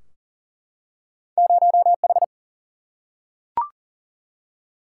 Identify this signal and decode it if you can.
Morse code, heard as a keyed tone, about 38 wpm, 700 Hz. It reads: 0H
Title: Single-Letter Single-Number 40wpm